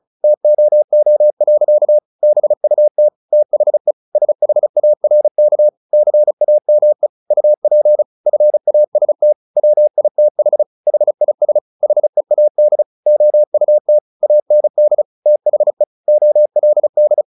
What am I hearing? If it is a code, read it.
Morse code, heard as a keyed tone, about 34 wpm, 600 Hz. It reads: TOO. BUT THE SHARK CAME UP FAST WITH HIS HEAD OUT AND THE OLD
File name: the old man and the sea 2 35wpm